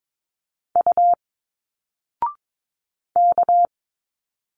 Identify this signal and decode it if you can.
Morse code, heard as a keyed tone, about 22 wpm, 700 Hz. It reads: U K
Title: Single Letters 22wpm